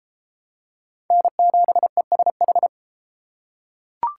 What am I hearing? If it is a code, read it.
Morse code, heard as a keyed tone, about 30 wpm, 700 Hz. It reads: N7ESH